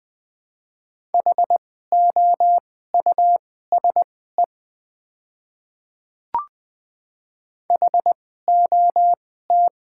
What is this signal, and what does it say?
Morse code, heard as a keyed tone, about 20 wpm, spaced as for 15 wpm, 690 Hz. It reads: HOUSE HOT
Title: 2x Character Spacing - Top 300 Words - 20wpm